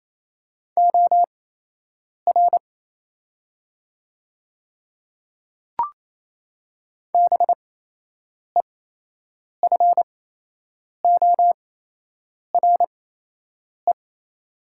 Morse code, heard as a keyed tone, about 28 wpm, spaced as for 8 wpm, 700 Hz. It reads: OR BEFORE
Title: 8x Character Spacing - Top 200 Words - 28wpm